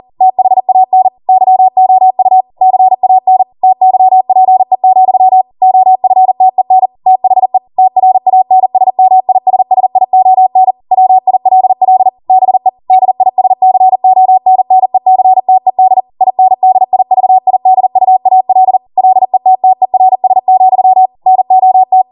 Morse code, heard as a keyed tone, about 40 wpm, 750 Hz. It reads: THAN YOU CAN TYPE, OFTEN THE TRANSMISSION WILL BE DISCONNECTED INDIVIDUAL LETTERS, NOT